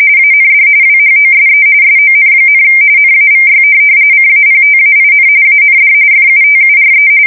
RTTY, about 45.45 baud. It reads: AND WHOSE SON, NAVY CAPTAIN AND FUTURE
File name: fsk170 / USS-New-Jersey